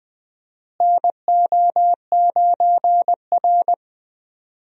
Morse code, heard as a keyed tone, about 20 wpm, 700 Hz. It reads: NO9R
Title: USA Call Signs - 4 Characters - Repeated 2x - 20wpm